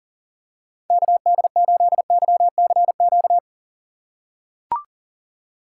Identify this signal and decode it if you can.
Morse code, heard as a keyed tone, about 40 wpm, 700 Hz. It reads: KD8YCQ
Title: USA Call Signs - 2x Character Spacing - 40wpm